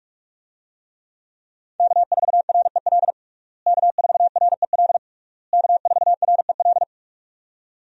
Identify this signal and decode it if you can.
Morse code, heard as a keyed tone, about 46 wpm, 700 Hz. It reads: K4REL K4REL K4REL
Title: USA Call Signs - 5 Characters - Repeated 3x - 45wpm